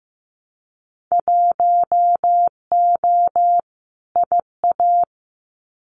Morse code, heard as a keyed tone, about 15 wpm, 700 Hz. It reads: 1O IA
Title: Winter Field Day Exchanges 15wpm